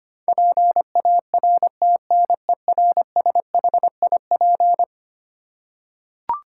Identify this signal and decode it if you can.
Morse code, heard as a keyed tone, about 25 wpm, 700 Hz. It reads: PARTNERSHIP
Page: morse-code-ninja-practice-25wpm